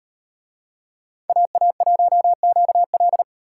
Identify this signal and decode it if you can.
Morse code, heard as a keyed tone, about 38 wpm, 700 Hz. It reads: AA1QL